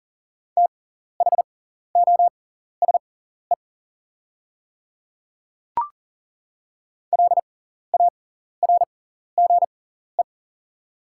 Morse code, heard as a keyed tone, about 40 wpm, spaced as for 14 wpm, 700 Hz. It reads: THOSE LARGE
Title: 6x Character Spacing - Top 500 Words - 40wpm